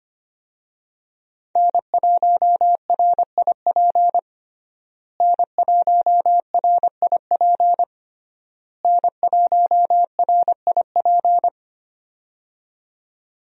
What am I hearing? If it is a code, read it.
Morse code, heard as a keyed tone, about 25 wpm, 700 Hz. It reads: N1RIP N1RIP N1RIP
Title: USA Call Signs - 5 Characters - Repeated 3x - 25wpm